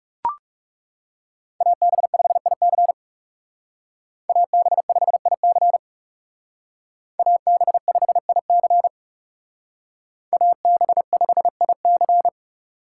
Morse code, heard as a keyed tone, about 38 wpm, 700 Hz. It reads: AB5IC AB5IC AB5IC AB5IC